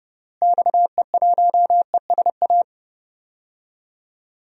Morse code, heard as a keyed tone, about 29 wpm, 700 Hz. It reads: XE1ESA
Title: DXCC-100 Call Signs - 6 Characters - Repeated 2x - 30wpm